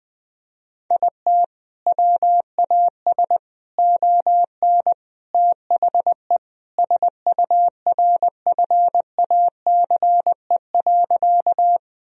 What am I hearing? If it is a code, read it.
Morse code, heard as a keyed tone, about 20 wpm, 700 Hz. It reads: IT WAS ON THE SURFACE.
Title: Sentences from Top 1000 Words - Review Part 3 - 20wpm